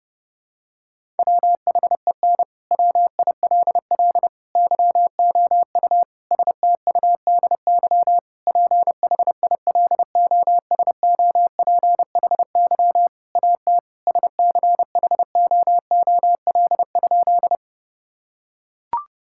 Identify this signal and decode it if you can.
Morse code, heard as a keyed tone, about 32 wpm, 700 Hz. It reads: WHEN WILL YOU STUDY PHILOSOPHY AT SCHOOL?
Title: Sentences from Top 1000 Words - Review Part 6 - 30wpm